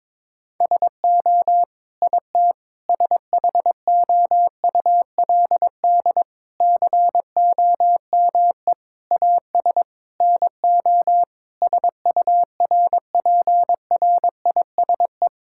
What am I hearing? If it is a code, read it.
Morse code, heard as a keyed tone, about 22 wpm, 700 Hz. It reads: SO IT SHOULD COME AS NO SURPRISE
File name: practice / 22wpm / Sets of 8 Words - 22wpm